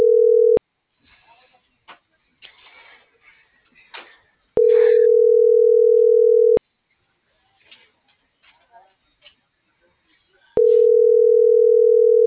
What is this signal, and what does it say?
Ringback tone (North America), 440 + 480 Hz.